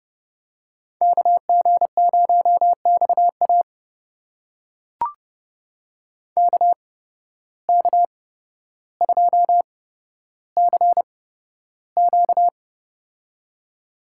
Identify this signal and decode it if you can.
Morse code, heard as a keyed tone, about 29 wpm, spaced as for 13 wpm, 700 Hz. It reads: KG0XA KK2CQ
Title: USA Call Signs - 5 Characters - 8x Character Spacing - 30wpm